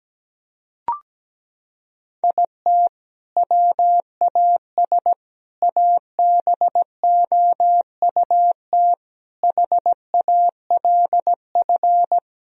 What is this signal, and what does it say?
Morse code, heard as a keyed tone, about 17 wpm, 700 Hz. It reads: IT WAS ABOUT HALF
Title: Sentences from Top 1000 Words - Review Part 3 - 17wpm